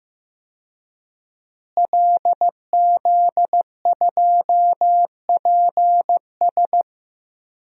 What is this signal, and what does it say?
Morse code, heard as a keyed tone, about 15 wpm, 700 Hz. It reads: LZ2PS
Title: DXCC-100 Call Signs - 5 Characters - Repeated 2x - 15wpm